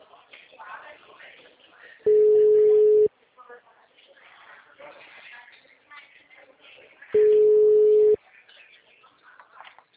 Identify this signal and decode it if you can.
Ringback tone (Europe, 425 Hz), with speech.